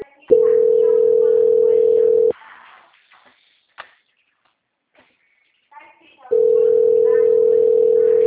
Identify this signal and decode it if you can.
Ringback tone (North America), 440 + 480 Hz, with speech.
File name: out- -2008- - - .28493